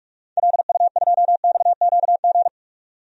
Morse code, heard as a keyed tone, about 45 wpm, 700 Hz. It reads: LU2XQG
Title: DXCC-100 Call Signs - 6 Characters - 45wpm